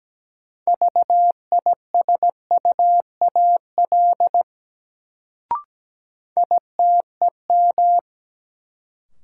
Morse code, heard as a keyed tone, about 17 wpm, 700 Hz. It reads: VISUAL ITEM
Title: Sentences from Top 1000 Words - Review Part 6 - New Word Drill - 17wpm